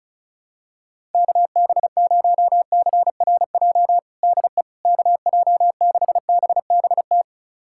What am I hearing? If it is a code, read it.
Morse code, heard as a keyed tone, about 36 wpm, 700 Hz. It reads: KB0CRJ DE KJ6BBT
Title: USA Callsign DE Callsign - 6 Characters - 35wpm